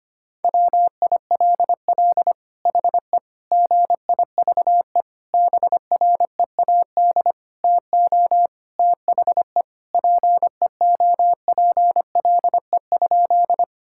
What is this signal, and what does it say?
Morse code, heard as a keyed tone, about 25 wpm, 700 Hz. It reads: WILL HE GIVE BREAD TO THE PEOPLE?
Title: Sentences from Top 1600 Words 25wpm